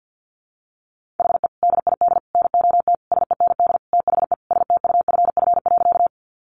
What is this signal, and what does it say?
Morse code, heard as a keyed tone, about 50 wpm, 700 Hz. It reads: HE DID NOT SEND THE STUFF.
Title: Sentences from Top 1500 Words 50wpm